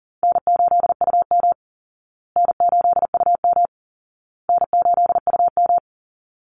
Morse code, heard as a keyed tone, about 40 wpm, 700 Hz. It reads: N8UM N8UM N8UM
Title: Call Signs Repeated 2x - 40wpm